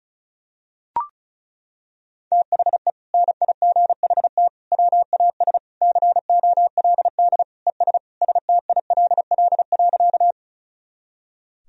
Morse code, heard as a keyed tone, about 36 wpm, 700 Hz. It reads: THE NIGHT WAS COLD ES STILL.
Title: Sentences from Top 400 Words and Abbreviations 35wpm